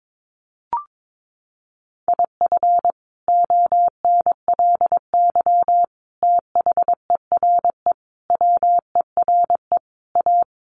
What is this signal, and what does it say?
Morse code, heard as a keyed tone, about 22 wpm, 700 Hz. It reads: IF ONLY THERE WERE A
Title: Sentences from Top 1400 Words 22wpm